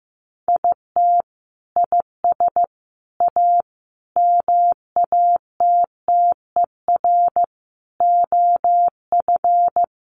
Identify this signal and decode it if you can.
Morse code, heard as a keyed tone, about 14 wpm, 700 Hz. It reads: IT IS A MATTER OF
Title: Sentences from Top 1400 Words 15wpm